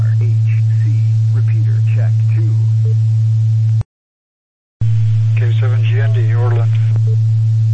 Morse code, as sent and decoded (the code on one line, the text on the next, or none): none